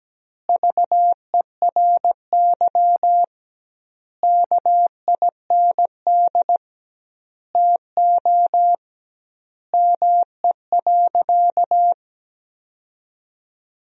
Morse code, heard as a keyed tone, about 17 wpm, 700 Hz. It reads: VERY KIND TO ME.
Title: 2x Word-Spacing - Sentences from Top 200 Words - 17wpm